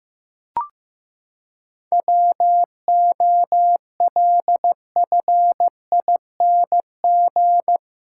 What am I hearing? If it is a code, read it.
Morse code, heard as a keyed tone, about 15 wpm, 700 Hz. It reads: WOLFING
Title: Pangram Challenge v2 - 15wpm